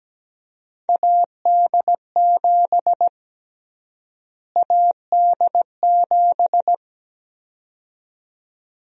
Morse code, heard as a keyed tone, about 17 wpm, 700 Hz. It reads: AD7 AD7
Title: USA Call Sign Prefixes - Repeated 2x - 17wpm